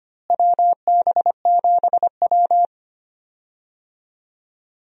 Morse code, heard as a keyed tone, about 25 wpm, 700 Hz. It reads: WB7W
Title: USA Call Signs - 4 Characters - Repeated 3x - 25wpm